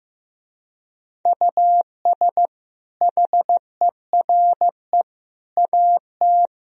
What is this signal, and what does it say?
Morse code, heard as a keyed tone, about 15 wpm, 700 Hz. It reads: US HERE AT
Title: Speed-Racing - Sets of 5 Words 15wpm